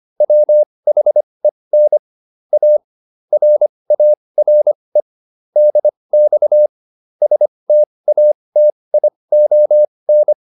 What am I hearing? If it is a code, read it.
Morse code, heard as a keyed tone, about 25 wpm, spaced as for 20 wpm, 600 Hz. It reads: WHEN A RARE DX STATION